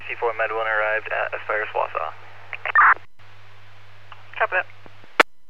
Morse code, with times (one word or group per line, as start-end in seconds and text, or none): none